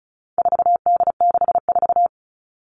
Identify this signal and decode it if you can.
Morse code, heard as a keyed tone, about 34 wpm, 700 Hz. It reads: 4D64
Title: All Cut Numbers - 4-Digit 35wpm